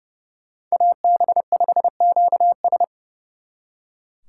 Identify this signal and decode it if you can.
Morse code, heard as a keyed tone, about 37 wpm, 700 Hz. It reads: AB5QS
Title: Speed Racing - USA 2x2 Call Signs 30wpm